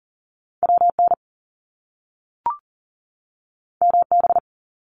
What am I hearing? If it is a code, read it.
Morse code, heard as a keyed tone, about 38 wpm, 700 Hz. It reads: WN MB
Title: Top Two Letter Combinations in Words 40wpm